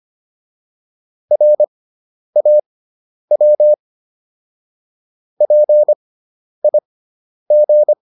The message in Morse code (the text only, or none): RAW PIG